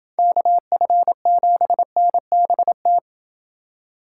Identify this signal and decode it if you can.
Morse code, heard as a keyed tone, about 27 wpm, 700 Hz. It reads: KF7NBT